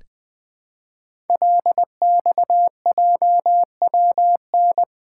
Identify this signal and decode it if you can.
Morse code, heard as a keyed tone, about 20 wpm, 700 Hz. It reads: LXJWN
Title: Code Groups - 5 Letter:Numbers - 20wpm